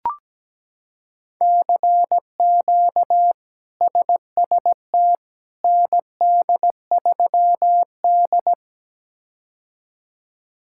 Morse code, heard as a keyed tone, about 17 wpm, 700 Hz. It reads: CQ SST ND3D
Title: K1USN SST QSO Practice - 17wpm